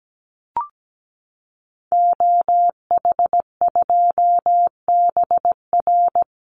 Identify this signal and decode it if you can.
Morse code, heard as a keyed tone, about 17 wpm, 700 Hz. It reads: OH2BR
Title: Call Signs Repeated 3x - 17wpm